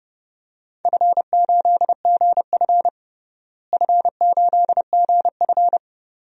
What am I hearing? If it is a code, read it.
Morse code, heard as a keyed tone, about 30 wpm, 700 Hz. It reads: F8GF F8GF
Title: DXCC-100 Call Signs - 4 Characters - Repeated 2x - 30wpm